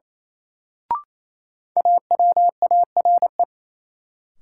Morse code, heard as a keyed tone, about 28 wpm, 700 Hz. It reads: AWARE
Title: Sentences from Top 1200 Words - New Word Drill Alt - 28wpm